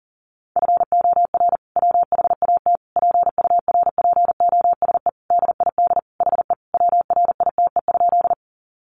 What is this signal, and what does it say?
Morse code, heard as a keyed tone, about 40 wpm, 700 Hz. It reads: FOR WHAT PURPOSE DID HE WRITE?
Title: Sentences from Top 800 Words 40wpm